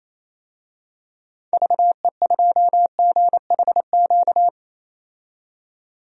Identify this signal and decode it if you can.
Morse code, heard as a keyed tone, about 28 wpm, 700 Hz. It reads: VE2GHQ
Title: DXCC-100 Call Signs - 6 Characters - 28wpm